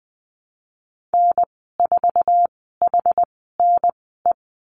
Morse code, heard as a keyed tone, about 20 wpm, spaced as for 14 wpm, 700 Hz. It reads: N4HNE
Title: 069 Lesson 12g - Letter d - Call Signs 20wpm